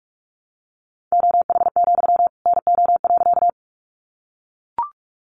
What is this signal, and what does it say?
Morse code, heard as a keyed tone, about 44 wpm, 700 Hz. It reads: OH, NO.
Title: Sentences from Top 500 Words 45wpm